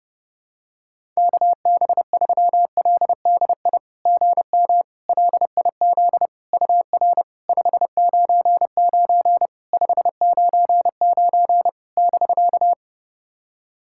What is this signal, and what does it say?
Morse code, heard as a keyed tone, about 30 wpm, 690 Hz. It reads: KB3LDI GM LIZ UR 599 599 <BK>
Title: SOTA QSO Practice 30wpm